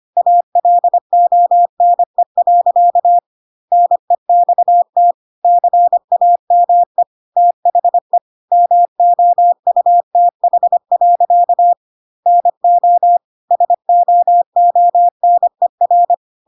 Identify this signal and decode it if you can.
Morse code, heard as a keyed tone, about 25 wpm, 700 Hz. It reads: ALONE. NEXT CAME THE MOUTH. NO SOONER
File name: Pinocchio / Pinocchio ch3 25 25 ik1hge